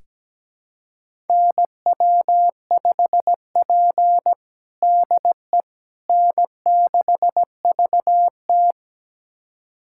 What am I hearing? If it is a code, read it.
Morse code, heard as a keyed tone, about 17 wpm, 700 Hz. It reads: NW5P DE N6VT
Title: USA Callsign DE Callsign - 4 Characters - 17wpm